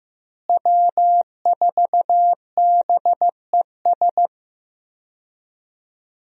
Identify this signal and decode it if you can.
Morse code, heard as a keyed tone, about 15 wpm, 700 Hz. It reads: W4BES
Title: USA Call Signs - 5 Characters - Repeated 2x - 15wpm